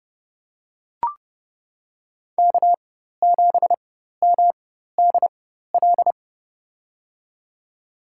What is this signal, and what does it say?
Morse code, heard as a keyed tone, about 31 wpm, spaced as for 14 wpm, 700 Hz. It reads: K7MDL
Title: USA Call Signs - 5 Characters - 4x Character Spacing - 30wpm